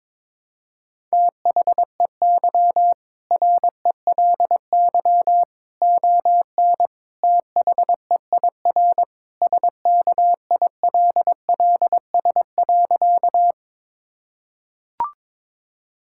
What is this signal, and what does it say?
Morse code, heard as a keyed tone, about 22 wpm, 700 Hz. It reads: THEY RELY ON THEIR SKILLS.
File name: practice / 22wpm / Sentences with 2-Word Phrases - 22wpm